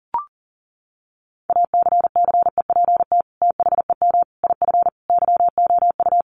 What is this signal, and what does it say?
Morse code, heard as a keyed tone, about 40 wpm, 700 Hz. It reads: ACCEPT THEM IF YOU
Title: Sentences from Top 1300 Words 40wpm